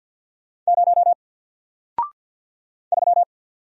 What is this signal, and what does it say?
Morse code, heard as a keyed tone, about 47 wpm, 700 Hz. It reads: 0 3
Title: Single Letter-Number Alt 50wpm